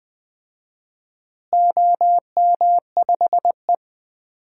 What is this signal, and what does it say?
Morse code, heard as a keyed tone, about 20 wpm, 700 Hz. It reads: OM5E
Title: DXCC-100 Call Signs - 4 Characters - Repeated 3x - 20wpm